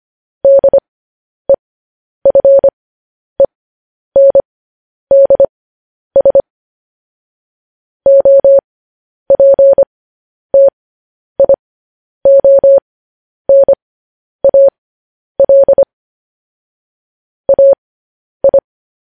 Morse code, heard as a keyed tone, about 25 wpm, spaced as for 10 wpm, 550 Hz. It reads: DEFENDS OPTIONAL AI
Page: news headlines in morse code at 10 wpm